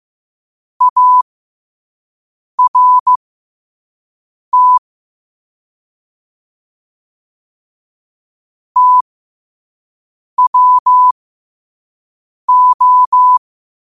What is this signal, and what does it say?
Morse code, heard as a keyed tone, about 15 wpm, spaced as for 5 wpm, 1000 Hz. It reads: ART TWO
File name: 5wpm